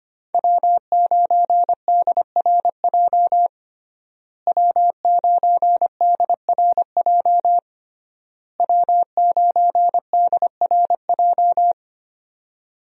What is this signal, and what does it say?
Morse code, heard as a keyed tone, about 25 wpm, 700 Hz. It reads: W9DRJ W9DRJ W9DRJ
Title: USA Call Signs - 5 Characters - Repeated 3x - 25wpm